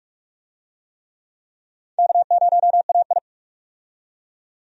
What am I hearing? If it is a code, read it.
Morse code, heard as a keyed tone, about 46 wpm, 700 Hz. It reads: K0AI
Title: USA Call Signs - 4 Characters - 45wpm